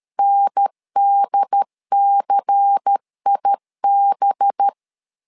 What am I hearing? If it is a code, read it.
Morse code, heard as a keyed tone, about 13 wpm, 800 Hz. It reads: NDCIB
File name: LEZIONI / 5 LBVC / 5 LBVC 12wpm 1955